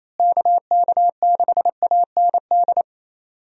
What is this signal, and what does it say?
Morse code, heard as a keyed tone, about 28 wpm, 700 Hz. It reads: KK6AND
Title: USA Call Signs - 6 Characters - Repeated 2x - 28wpm